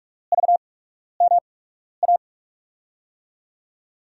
Morse code, heard as a keyed tone, about 42 wpm, spaced as for 12 wpm, 700 Hz. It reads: VMA
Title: USA Call Signs - 6 Characters - 8x Character Spacing - 45wpm